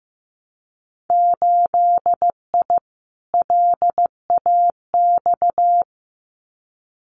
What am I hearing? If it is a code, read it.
Morse code, heard as a keyed tone, about 15 wpm, 700 Hz. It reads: 8I LAX
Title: Winter Field Day Exchanges Encore 15wpm